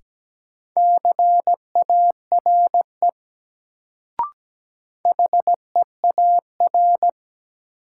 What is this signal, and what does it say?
Morse code, heard as a keyed tone, about 17 wpm, 700 Hz. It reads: CARE HEAR
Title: Sentences from Top 300 Words - New Word Drill - 17wpm